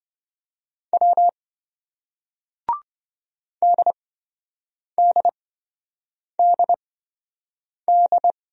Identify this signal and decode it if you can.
Morse code, heard as a keyed tone, about 28 wpm, spaced as for 7 wpm, 700 Hz. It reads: W DDDD